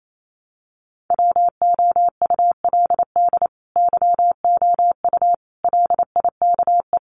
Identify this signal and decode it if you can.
Morse code, heard as a keyed tone, about 28 wpm, 700 Hz. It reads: WOULD YOU LIKE
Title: Sentences from Top 1000 Words 28wpm